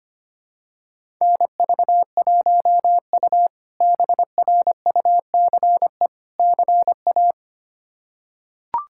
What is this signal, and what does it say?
Morse code, heard as a keyed tone, about 25 wpm, 700 Hz. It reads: NV1U BRUCE CA
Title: K1USN SST QSO Practice - 25wpm